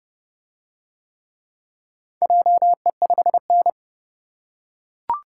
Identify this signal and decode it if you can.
Morse code, heard as a keyed tone, about 28 wpm, 700 Hz. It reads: JE5N